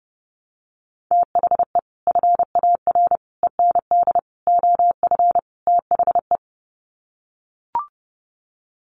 Morse code, heard as a keyed tone, about 30 wpm, 700 Hz. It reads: THE FAR END OF THE
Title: Sets of 5 Words 30wpm